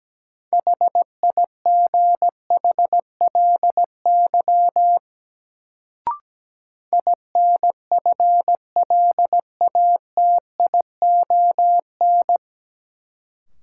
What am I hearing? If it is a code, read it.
Morse code, heard as a keyed tone, about 17 wpm, 700 Hz. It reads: HIGHLY INFLATION
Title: Sentences from Top 1100 Words - New Word Drill - 17wpm